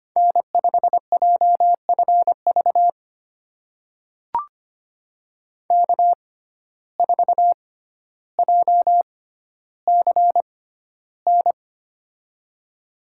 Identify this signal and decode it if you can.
Morse code, heard as a keyed tone, about 25 wpm, spaced as for 13 wpm, 700 Hz. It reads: N5JFV K4JCN
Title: USA Call Signs - 5 Characters - 6x Character Spacing - 25wpm